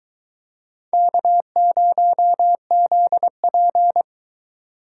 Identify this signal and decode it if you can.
Morse code, heard as a keyed tone, about 21 wpm, 700 Hz. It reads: K0ZP